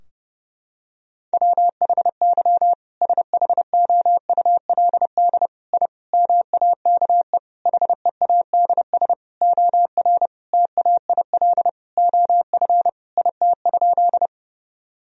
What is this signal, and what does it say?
Morse code, heard as a keyed tone, about 32 wpm, 700 Hz. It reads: WHY SHOULD I MAKE HEADS OR TAIL OF IT?
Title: Sentences from Top 1000 Words - Review Part 3 - 30wpm